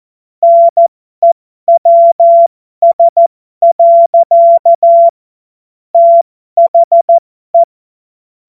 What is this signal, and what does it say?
Morse code, heard as a keyed tone, about 14 wpm, spaced as for 12 wpm, 680 Hz. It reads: NEWS. THE
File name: Morsekurs / Witze